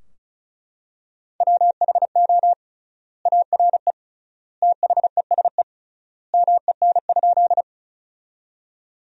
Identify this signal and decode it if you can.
Morse code, heard as a keyed tone, about 36 wpm, 700 Hz. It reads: WHO ARE THESE MEN?
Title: 3x Word-Spacing - Sentences from Top 200 Words - 35wpm